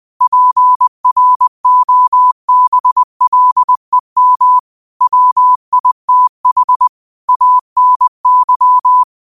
Morse code, heard as a keyed tone, about 20 wpm, 1000 Hz. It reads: PROBLEM WITH ANY